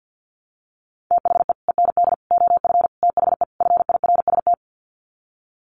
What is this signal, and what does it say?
Morse code, heard as a keyed tone, about 49 wpm, 700 Hz. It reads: THE END OF THE FIRST
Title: Sets of 5 Words Encore 50wpm